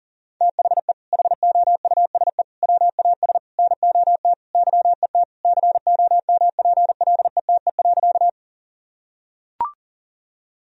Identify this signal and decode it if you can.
Morse code, heard as a keyed tone, about 40 wpm, 700 Hz. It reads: THE HOUSE WAS NOT YET COMPLETE.
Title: Sentences from Top 1000 Words - Review Part 2 - 40wpm